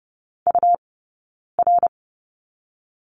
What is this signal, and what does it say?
Morse code, heard as a keyed tone, about 30 wpm, 700 Hz. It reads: U R
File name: course / 30:10wpm / 099 Lesson 17d - Letter u - Two Characters 30wpm